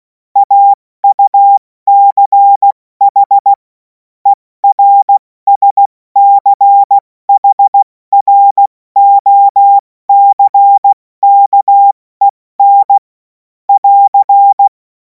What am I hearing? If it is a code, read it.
Morse code, heard as a keyed tone, about 16 wpm, spaced as for 14 wpm, 800 Hz. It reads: AUCH ERSCHROCKEN <AR>
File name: Morsekurs / Witze / Witz 17. Februar 2026 70 F80